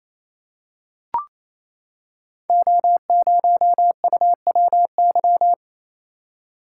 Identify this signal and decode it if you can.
Morse code, heard as a keyed tone, about 29 wpm, 700 Hz. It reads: O0UWY
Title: Code Groups - 5 Letter:Numbers - 28wpm